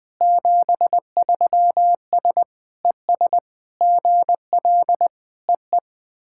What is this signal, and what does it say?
Morse code, heard as a keyed tone, about 20 wpm, 700 Hz. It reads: 73S ES GL EE
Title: SOTA QSO Practice 20wpm